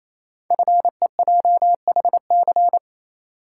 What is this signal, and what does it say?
Morse code, heard as a keyed tone, about 28 wpm, 700 Hz. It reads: FEJHC
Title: Code Groups - 5 Letters - 28wpm